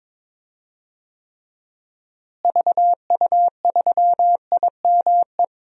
Morse code, heard as a keyed tone, about 22 wpm, 700 Hz. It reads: VU3IME
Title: DXCC-100 Call Signs - 6 Characters - Repeated 3x - 22wpm